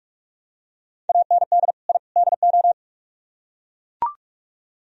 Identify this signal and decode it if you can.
Morse code, heard as a keyed tone, about 42 wpm, 700 Hz. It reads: AND I DO